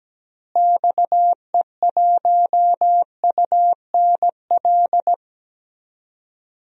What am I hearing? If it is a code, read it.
Morse code, heard as a keyed tone, about 17 wpm, 700 Hz. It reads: XE1UNL
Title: DXCC-100 Call Signs - 6 Characters - Repeated 3x - 17wpm